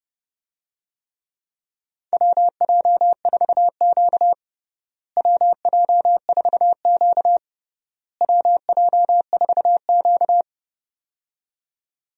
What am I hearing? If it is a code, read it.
Morse code, heard as a keyed tone, about 30 wpm, 700 Hz. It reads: WJ4Q WJ4Q WJ4Q
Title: USA Call Signs - 4 Characters - Repeated 3x - 30wpm